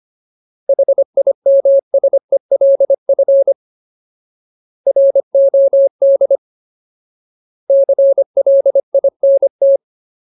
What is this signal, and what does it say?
Morse code, heard as a keyed tone, about 25 wpm, 550 Hz. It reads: HIMSELF ROD CLINT
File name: Ultimate Word List / 25WPM Ultimate